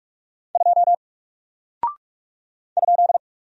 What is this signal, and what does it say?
Morse code, heard as a keyed tone, about 44 wpm, 700 Hz. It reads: 2 ?
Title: Single Character (Letter:Number:Punctuation:Prosign) Alt 45wpm